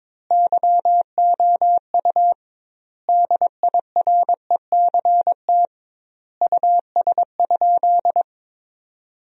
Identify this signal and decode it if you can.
Morse code, heard as a keyed tone, about 22 wpm, 700 Hz. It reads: YOU DIRECT US?
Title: 2x Word-Spacing - Sentences from Top 400 Words - 22wpm